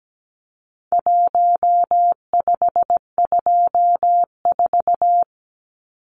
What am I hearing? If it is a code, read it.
Morse code, heard as a keyed tone, about 17 wpm, 700 Hz. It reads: 1524
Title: Numbers - 4-Digit 17wpm